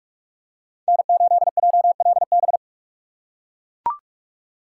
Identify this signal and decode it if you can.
Morse code, heard as a keyed tone, about 41 wpm, 700 Hz. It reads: N8JLB